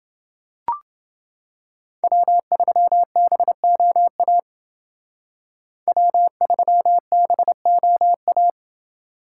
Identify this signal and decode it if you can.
Morse code, heard as a keyed tone, about 28 wpm, 700 Hz. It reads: W3BOA W3BOA